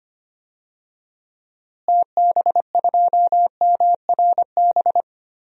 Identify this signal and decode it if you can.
Morse code, heard as a keyed tone, about 25 wpm, 700 Hz. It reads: TB2MRB
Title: DXCC-100 Call Signs - 6 Characters - Repeated 2x - 25wpm